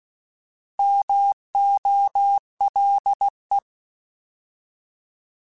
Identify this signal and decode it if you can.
Morse code, heard as a keyed tone, about 16 wpm, 800 Hz. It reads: MOLE
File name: Alphabetical / Alphabetical mp3s / Ms @15